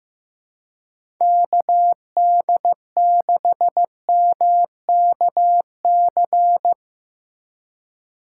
Morse code, heard as a keyed tone, about 15 wpm, 700 Hz. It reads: KD6MKC
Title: USA Call Signs - 6 Characters - 2x Character Spacing - 15wpm